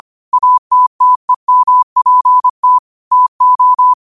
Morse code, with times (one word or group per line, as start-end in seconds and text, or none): 0.3-2.8 ATTEMPT
3.1-3.9 TO